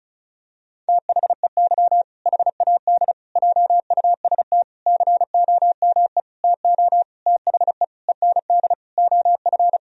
Morse code, heard as a keyed tone, about 36 wpm, 700 Hz. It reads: THEY HAD JUST COME TO THE END OF
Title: Sentences from Top 1000 Words - Review Part 2 - 35wpm